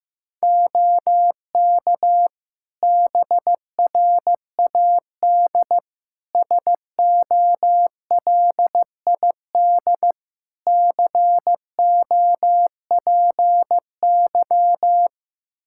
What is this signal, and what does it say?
Morse code, heard as a keyed tone, about 16 wpm, 700 Hz. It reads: OK BRAD SOLID COPY
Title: Practice QSO - Exchange 3 - More Info - 15wpm